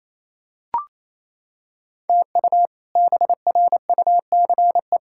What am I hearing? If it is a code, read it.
Morse code, heard as a keyed tone, about 28 wpm, 700 Hz. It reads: TU BRUCE
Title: K1USN SST QSO Practice - 28wpm